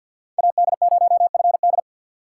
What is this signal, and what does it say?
Morse code, heard as a keyed tone, about 50 wpm, 700 Hz. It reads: AD0FD
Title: USA Call Signs - 4x Character Spacing - 50wpm